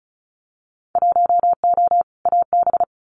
Morse code, heard as a keyed tone, about 36 wpm, 700 Hz. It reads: 1O AB
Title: Winter Field Day Exchanges Encore 35wpm